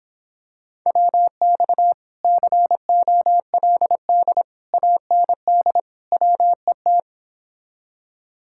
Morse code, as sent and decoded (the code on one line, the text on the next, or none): .-- -..- / -.-. --- .-.. -.. / .- -. -.. / .-- . -
WX COLD AND WET